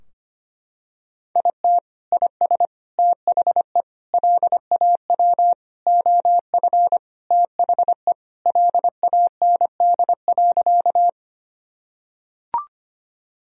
Morse code, heard as a keyed tone, about 25 wpm, 700 Hz. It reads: IT IS THE LAW OF THE LAND.
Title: Sentences from Top 1000 Words - Review Part 3 - 25wpm